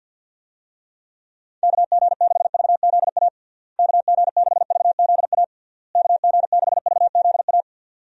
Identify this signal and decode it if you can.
Morse code, heard as a keyed tone, about 47 wpm, 700 Hz. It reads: KG6VZA KG6VZA KG6VZA
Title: USA Call Signs - 6 Characters - Repeated 3x - 50wpm